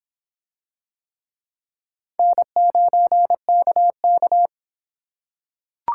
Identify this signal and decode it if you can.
Morse code, heard as a keyed tone, about 24 wpm, 700 Hz. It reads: N9KK